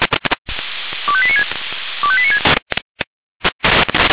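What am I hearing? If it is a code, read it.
Five-tone selective call (ZVEI), heard as quick runs of five tones, tones of about 80 ms. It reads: 25896 25896
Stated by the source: Received by SDR, ZVEI five-tone selective call.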